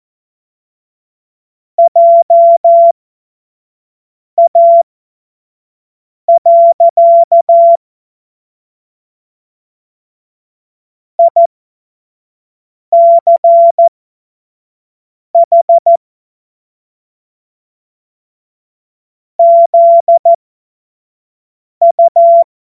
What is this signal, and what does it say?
Morse code, heard as a keyed tone, about 14 wpm, spaced as for 5 wpm, 680 Hz. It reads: JA. ICH ZU
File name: Morsekurs / Witze / Witz 15. Februar 2022 25 F70